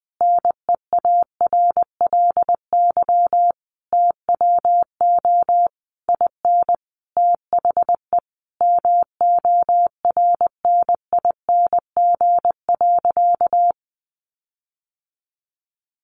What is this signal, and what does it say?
Morse code, heard as a keyed tone, about 20 wpm, 700 Hz. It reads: NEARLY TWO IN THE MORNING.
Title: Sentences from Top 1300 Words 20wpm